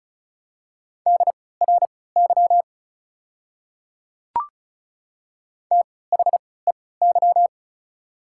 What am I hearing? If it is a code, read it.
Morse code, heard as a keyed tone, about 36 wpm, spaced as for 20 wpm, 700 Hz. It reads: DRY THEY
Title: 3x Character Spacing - Top 500 Words - 35wpm